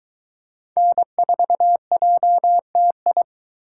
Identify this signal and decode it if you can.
Morse code, heard as a keyed tone, about 21 wpm, 700 Hz. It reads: N4JTI